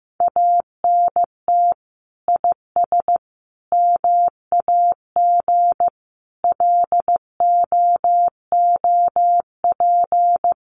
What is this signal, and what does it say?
Morse code, heard as a keyed tone, about 15 wpm, 700 Hz. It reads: ANT IS MAG LOOP
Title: Common QSO Phrases 15wpm